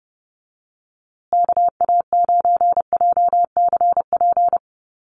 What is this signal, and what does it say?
Morse code, heard as a keyed tone, about 30 wpm, 700 Hz. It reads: KA9JCP
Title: USA Call Signs Repeated 2x 30wpm